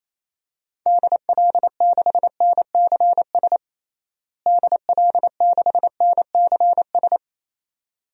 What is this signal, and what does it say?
Morse code, heard as a keyed tone, about 27 wpm, 700 Hz. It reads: DL6NCS DL6NCS
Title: DXCC-100 Call Signs - 6 Characters - Repeated 2x - 28wpm